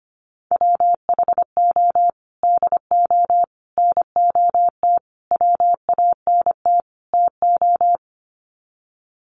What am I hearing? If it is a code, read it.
Morse code, heard as a keyed tone, about 25 wpm, 700 Hz. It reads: WHO DO NOT WANT TO
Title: Sets of 5 Words Encore2 25wpm